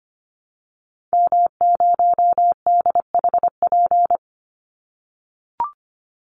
Morse code, heard as a keyed tone, about 24 wpm, 700 Hz. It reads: M0DHP
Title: Call Signs - 25wpm